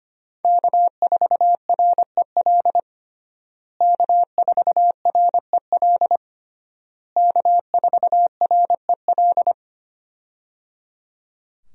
Morse code, heard as a keyed tone, about 25 wpm, 700 Hz. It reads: K4REL K4REL K4REL
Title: USA Call Signs - 5 Characters - Repeated 3x - 25wpm